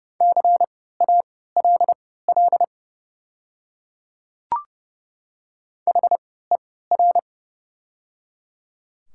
Morse code, heard as a keyed tone, about 30 wpm, spaced as for 17 wpm, 700 Hz. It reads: CALL HER
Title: 3x Character Spacing - Top 200 Words - 30wpm